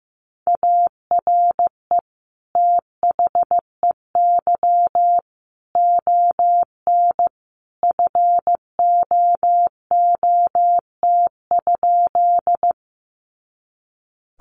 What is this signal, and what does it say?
Morse code, heard as a keyed tone, about 15 wpm, 700 Hz. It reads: ARE THEY ON FOOT?
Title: Sentences from Top 500 Words 15wpm